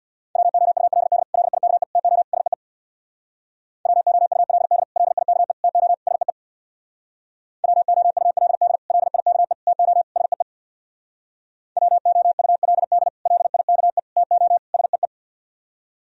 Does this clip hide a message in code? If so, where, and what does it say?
Yes: Morse code from 0.4 to 15.1 s, heard as a keyed tone, about 64 wpm, 700 Hz. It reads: WOULD LIKE TO SEE WOULD LIKE TO SEE WOULD LIKE TO SEE WOULD LIKE TO SEE